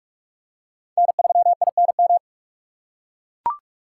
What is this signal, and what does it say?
Morse code, heard as a keyed tone, about 44 wpm, 700 Hz. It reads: N3INM